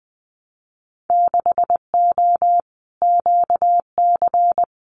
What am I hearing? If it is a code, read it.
Morse code, heard as a keyed tone, about 20 wpm, 700 Hz. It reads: 6O QC
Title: Winter Field Day Exchanges Encore 20wpm